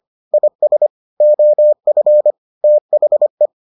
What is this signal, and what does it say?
Morse code, heard as a keyed tone, about 25 wpm, 600 Hz. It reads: IS OF THE